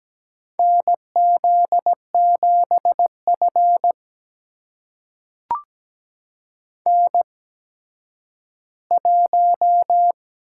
Morse code, heard as a keyed tone, about 17 wpm, spaced as for 10 wpm, 700 Hz. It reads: NZ7F N1
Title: USA Call Signs - 4 Characters - 8x Character Spacing - 17wpm